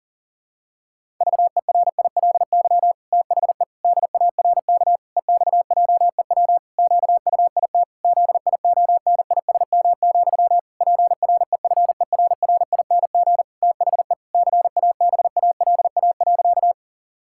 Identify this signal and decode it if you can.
Morse code, heard as a keyed tone, about 40 wpm, 700 Hz. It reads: VERILY THE DARK EXJEW QUIT ZIONISM, PREFERRING THE CABALA.
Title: Pangram Challenge v2 - 40wpm